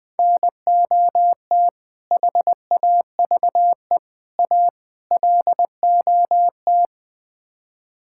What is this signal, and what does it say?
Morse code, heard as a keyed tone, about 20 wpm, 700 Hz. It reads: NOT HAVE A LOT
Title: Build A Phrase - 5-Word Phrases - 20wpm